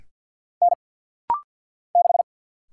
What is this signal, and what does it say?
Morse code, heard as a keyed tone, about 50 wpm, 700 Hz. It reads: N 6
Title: Single Letter-Number - Rapid-Fire 50wpm